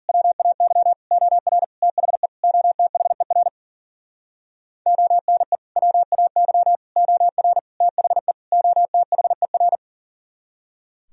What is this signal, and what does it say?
Morse code, heard as a keyed tone, about 52 wpm, 700 Hz. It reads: WAY OR THE OTHER ONE WAY OR THE OTHER